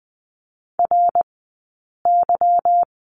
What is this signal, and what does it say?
Morse code, heard as a keyed tone, about 20 wpm, spaced as for 8 wpm, 700 Hz. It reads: RY
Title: 211 Lesson 34d - Number 8 - Two Characters 20wpm